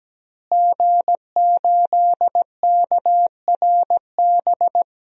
Morse code, heard as a keyed tone, about 17 wpm, 700 Hz. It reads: G8KRB
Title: DXCC-100 Call Signs - 5 Characters - Repeated 2x - 17wpm